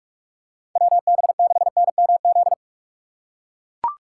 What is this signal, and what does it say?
Morse code, heard as a keyed tone, about 44 wpm, 700 Hz. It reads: WB6NMZ
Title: USA Call Signs - 6 Characters - 6x Character Spacing - 45wpm